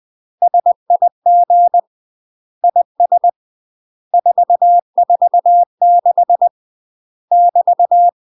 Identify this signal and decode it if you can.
Morse code, heard as a keyed tone, about 20 wpm, 700 Hz. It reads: SIG IS 446 <BT>
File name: UK MASTER QSOs TRAINING / 20WPM / UK MASTER QSOs 0015